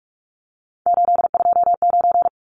Morse code, heard as a keyed tone, about 44 wpm, 700 Hz. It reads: 829
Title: Numbers - 3-Digit 45wpm